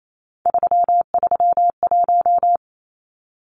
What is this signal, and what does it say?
Morse code, heard as a keyed tone, about 28 wpm, 700 Hz. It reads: 331
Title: Numbers - 3-Digit 28wpm